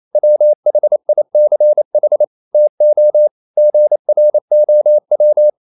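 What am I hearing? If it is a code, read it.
Morse code, heard as a keyed tone, about 28 wpm, 600 Hz. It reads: WHICH TO GROW
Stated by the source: Morse code practice at 28 wpm, text of Ken Follett's novel Never in English, from CW med Gnister.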